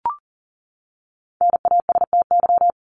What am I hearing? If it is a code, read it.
Morse code, heard as a keyed tone, about 40 wpm, 700 Hz. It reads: NASTY
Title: Top 1751-2000 Words 40wpm